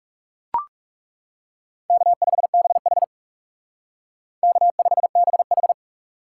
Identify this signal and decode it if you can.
Morse code, heard as a keyed tone, about 39 wpm, 700 Hz. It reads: K5BH K5BH